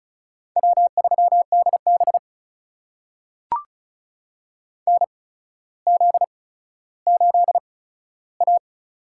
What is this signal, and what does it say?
Morse code, heard as a keyed tone, about 36 wpm, spaced as for 15 wpm, 700 Hz. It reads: W3DB NZ8A
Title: USA Call Signs - 8x Character Spacing - 35wpm